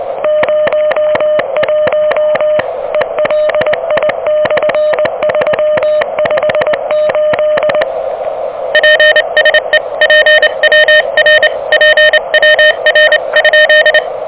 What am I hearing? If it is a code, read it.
Morse code, heard as a keyed tone, about 25 wpm, 600 Hz, with noise. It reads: 01 ELI/358 PSE PWR PWR?
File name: OK QRP 2023 / OK QRP 2023